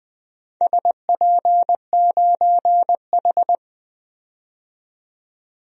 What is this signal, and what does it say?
Morse code, heard as a keyed tone, about 20 wpm, 700 Hz. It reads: SP9H
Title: DXCC-100 Call Signs - 4 Characters - 20wpm